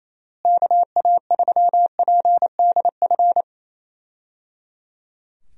Morse code, heard as a keyed tone, about 28 wpm, 700 Hz. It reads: KA3PDF
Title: USA Call Signs - 6 Characters - Repeated 3x - 28wpm